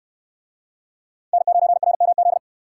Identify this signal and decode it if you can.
Morse code, heard as a keyed tone, about 68 wpm, 700 Hz. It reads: N9RMZ